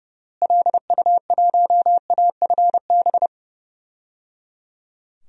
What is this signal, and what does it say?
Morse code, heard as a keyed tone, about 30 wpm, 700 Hz. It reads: LU1AFB
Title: DXCC-100 Call Signs - 6 Characters - Repeated 3x - 30wpm